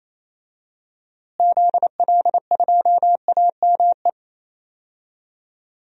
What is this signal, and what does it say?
Morse code, heard as a keyed tone, about 27 wpm, 700 Hz. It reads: ZL2AME
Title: DXCC-100 Call Signs - 6 Characters - Repeated 2x - 28wpm